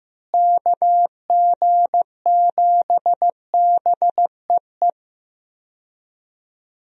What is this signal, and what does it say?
Morse code, heard as a keyed tone, about 15 wpm, 700 Hz. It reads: KG7BEE
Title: USA Call Signs - 6 Characters - Repeated 3x - 15wpm